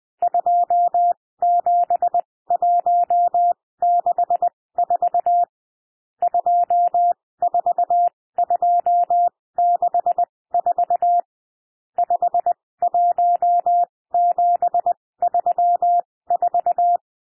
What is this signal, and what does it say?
Morse code, heard as a keyed tone, about 20 wpm, spaced as for 16 wpm, 700 Hz. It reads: 27164 24264 51734
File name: cwmp3 / N1 SOUND NUMBERS 20 wpm